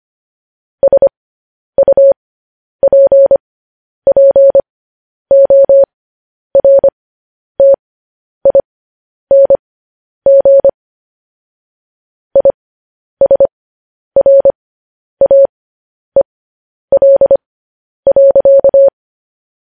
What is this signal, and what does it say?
Morse code, heard as a keyed tone, about 25 wpm, spaced as for 10 wpm, 550 Hz. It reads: SUPPORTING ISRAEL.
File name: podcast / News headlines-1x-25x10